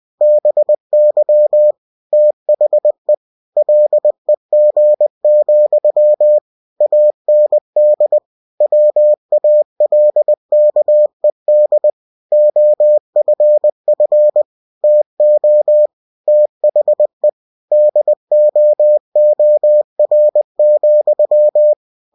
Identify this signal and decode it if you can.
Morse code, heard as a keyed tone, about 20 wpm, 600 Hz. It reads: BY THE LEG, AND WALKED OFF TO THE DOOR,